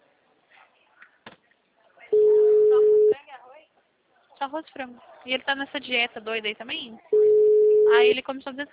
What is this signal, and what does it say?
Ringback tone (Europe, 425 Hz), with speech.
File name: out- -2003- - - .64579